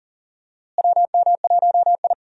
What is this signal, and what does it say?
Morse code, heard as a keyed tone, about 37 wpm, 700 Hz. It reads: WM1I
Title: USA Call Signs - 4 Characters - Repeated 3x - 40wpm